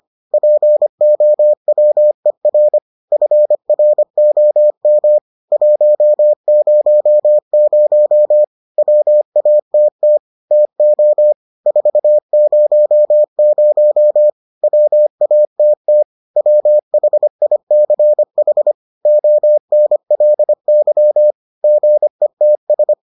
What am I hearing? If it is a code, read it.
Morse code, heard as a keyed tone, about 25 wpm, 600 Hz. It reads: POWER FROM 100 WATT TO 400 WATT WHICH ONLY GETS